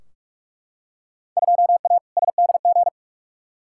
Morse code, heard as a keyed tone, about 41 wpm, 700 Hz. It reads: 2A SDG
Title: Field Day Exchanges Encore - 45wpm